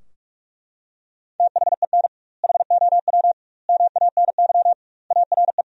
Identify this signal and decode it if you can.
Morse code, heard as a keyed tone, about 43 wpm, 700 Hz. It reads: THEN HOW MANY ARE
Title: 2x Word-Spacing - Sentences from Top 100 Words - 45wpm